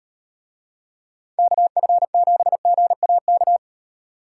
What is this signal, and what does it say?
Morse code, heard as a keyed tone, about 36 wpm, 700 Hz. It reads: KF7GAK